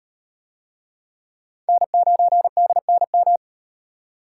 Morse code, heard as a keyed tone, about 37 wpm, 700 Hz. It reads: N9DNM